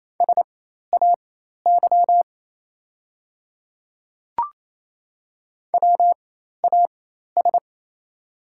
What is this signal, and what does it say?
Morse code, heard as a keyed tone, about 27 wpm, spaced as for 13 wpm, 690 Hz. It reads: SAY WAS
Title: 4x Character Spacing - Top 200 Words - 28wpm